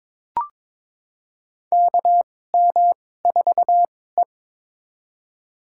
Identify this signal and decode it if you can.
Morse code, heard as a keyed tone, about 21 wpm, spaced as for 16 wpm, 700 Hz. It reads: KM4E
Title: USA Call Signs - 4 Characters - 2x Character Spacing - 22wpm